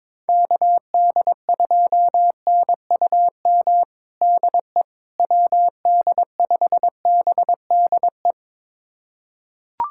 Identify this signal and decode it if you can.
Morse code, heard as a keyed tone, about 22 wpm, 700 Hz. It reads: KD2NUM DE WD5BDE
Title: USA Callsign DE Callsign 22wpm